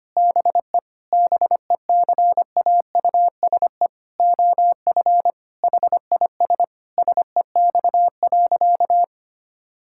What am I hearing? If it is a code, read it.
Morse code, heard as a keyed tone, about 25 wpm, 700 Hz. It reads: BE BECAUSE OF HIS SEX.
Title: Sentences from Top 1600 Words 25wpm